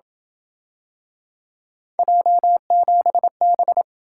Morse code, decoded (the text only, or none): J7B